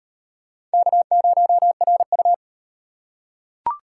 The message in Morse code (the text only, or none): K0RU